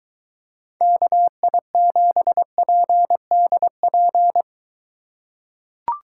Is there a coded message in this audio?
Yes: Morse code from 0.8 to 4.4 s, heard as a keyed tone, about 22 wpm, 700 Hz. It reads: KI7PDP